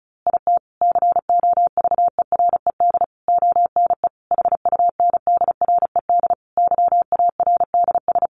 Morse code, heard as a keyed tone, about 34 wpm, 700 Hz. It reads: IT COVERED ONE HUNDRED YARDS
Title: A Sound Of Thunder 35wpm